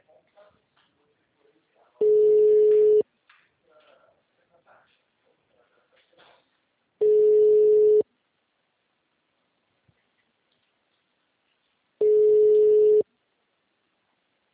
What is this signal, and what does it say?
Ringback tone (Europe, 425 Hz).